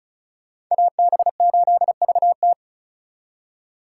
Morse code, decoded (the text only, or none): AB8VT